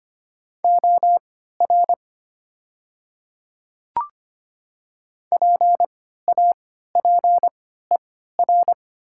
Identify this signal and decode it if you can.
Morse code, heard as a keyed tone, about 25 wpm, spaced as for 14 wpm, 700 Hz. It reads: OR PAPER
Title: 3x Character Spacing - Top 300 Words - 25wpm